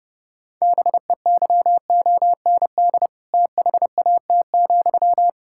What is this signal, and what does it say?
Morse code, heard as a keyed tone, about 32 wpm, 700 Hz. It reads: BEYOND THAT,
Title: Sentences from Top 1000 Words - Review Part 6 - 30wpm